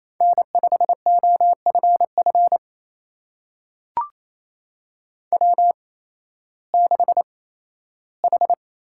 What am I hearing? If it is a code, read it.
Morse code, heard as a keyed tone, about 28 wpm, spaced as for 15 wpm, 700 Hz. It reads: N5OFF W6H
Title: USA Call Signs - 8x Character Spacing - 28wpm